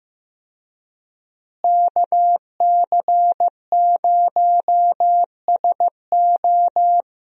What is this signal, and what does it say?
Morse code, heard as a keyed tone, about 15 wpm, 700 Hz. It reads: KC0SO
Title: USA Call Signs - 5 Characters - Repeated 2x - 15wpm